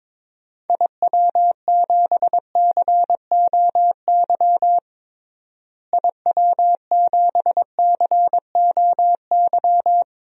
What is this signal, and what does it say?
Morse code, heard as a keyed tone, about 22 wpm, 700 Hz. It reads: IW7COY IW7COY
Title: DXCC-100 Call Signs - 6 Characters - Repeated 2x - 22wpm